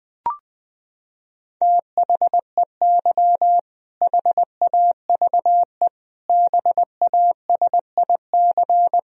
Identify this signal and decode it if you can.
Morse code, heard as a keyed tone, about 20 wpm, 700 Hz. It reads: THEY HAVE BASIC
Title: Sentences from Top 1000 Words - Review Part 3 - 20wpm